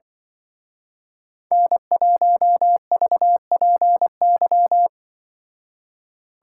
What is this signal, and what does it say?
Morse code, heard as a keyed tone, about 23 wpm, 700 Hz. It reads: N1VPY
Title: Speed Racing - USA 1x3 Call Signs 20wpm